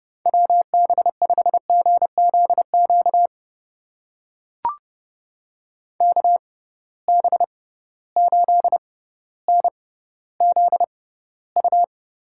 Morse code, heard as a keyed tone, about 31 wpm, spaced as for 16 wpm, 700 Hz. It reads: WB5GZQ KB8NZU
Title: USA Call Signs - 6 Characters - 6x Character Spacing - 30wpm